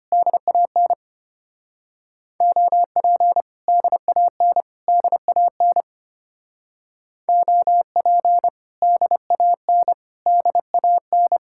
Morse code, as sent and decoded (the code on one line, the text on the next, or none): -.. .- -. / --- .--. / -.. .- -. / -.. .- -. / --- .--. / -.. .- -. / -.. .- -.
DAN OP DAN DAN OP DAN DAN